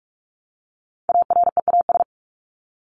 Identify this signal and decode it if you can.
Morse code, heard as a keyed tone, about 44 wpm, 700 Hz. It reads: AREAS
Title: ES:S Suffix - Top 100 Words with Base Words 45wpm